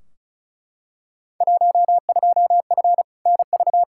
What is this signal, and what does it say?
Morse code, heard as a keyed tone, about 34 wpm, 700 Hz. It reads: 12F NV
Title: Field Day Exchanges Encore - 35wpm